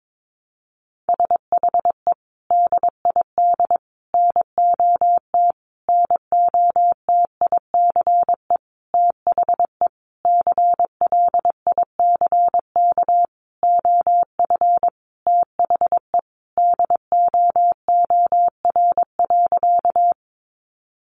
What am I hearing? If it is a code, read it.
Morse code, heard as a keyed tone, about 22 wpm, 700 Hz. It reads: SHE DID NOT NOTICE THE CLICK OF THE DOOR.
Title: Sentences from Top 1100 Words 22wpm